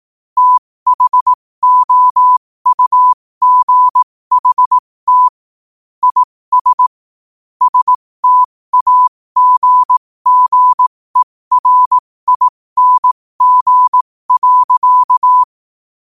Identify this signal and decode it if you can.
Morse code, heard as a keyed tone, about 18 wpm, spaced as for 15 wpm, 1000 Hz. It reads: THOUGHT IS STAGGERING.
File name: qotdfiles / 15wpm